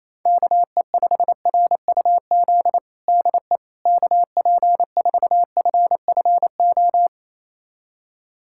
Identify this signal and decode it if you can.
Morse code, heard as a keyed tone, about 28 wpm, 700 Hz. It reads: KE5RUZ DE KP4FFO
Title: USA Callsign DE Callsign - 6 Characters - 28wpm